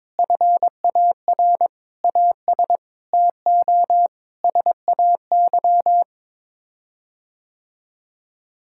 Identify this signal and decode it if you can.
Morse code, heard as a keyed tone, about 22 wpm, 700 Hz. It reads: FAR AS TO SAY
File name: practice / 22wpm / Sets of 8 Words - 22wpm